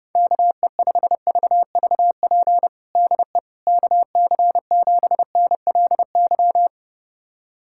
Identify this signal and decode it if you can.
Morse code, heard as a keyed tone, about 32 wpm, 700 Hz. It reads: KE5VVP DE KC7NLY
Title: USA Callsign DE Callsign 30wpm